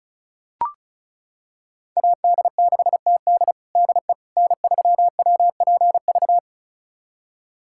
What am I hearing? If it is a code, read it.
Morse code, heard as a keyed tone, about 34 wpm, 700 Hz. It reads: AD6TD DE N3WPV
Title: USA Callsign DE Callsign - 5 Characters - 35wpm